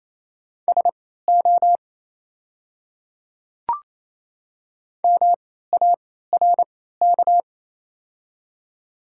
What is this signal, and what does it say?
Morse code, heard as a keyed tone, about 28 wpm, spaced as for 16 wpm, 700 Hz. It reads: SO MARK
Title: 3x Character Spacing - Top 300 Words - 28wpm